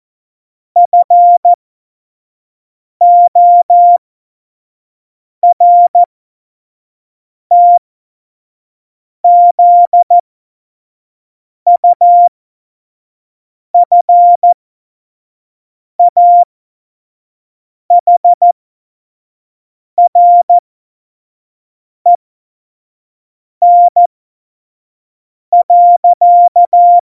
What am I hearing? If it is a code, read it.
Morse code, heard as a keyed tone, about 14 wpm, spaced as for 5 wpm, 700 Hz. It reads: FORTZUFAHREN.
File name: Morsekurs / Witze / Witz 29. Januar 2019, 25 F70